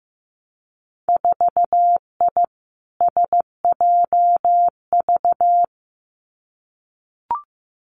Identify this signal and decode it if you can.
Morse code, heard as a keyed tone, about 16 wpm, 700 Hz. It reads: 4I SJV
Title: Winter Field Day Exchanges Encore 15wpm